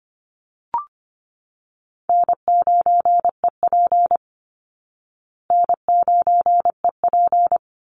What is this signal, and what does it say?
Morse code, heard as a keyed tone, about 25 wpm, 700 Hz. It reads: N9EP N9EP
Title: Easy Call Signs Repeated 2x - 25wpm